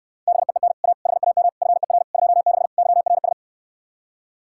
Morse code, heard as a keyed tone, about 69 wpm, 700 Hz. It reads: BEEN A HAM FER 26 YRS